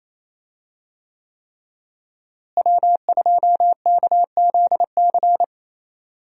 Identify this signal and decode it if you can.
Morse code, heard as a keyed tone, about 28 wpm, 700 Hz. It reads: W2KZC
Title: USA Call Signs - 5 Characters - Repeated 3x - 28wpm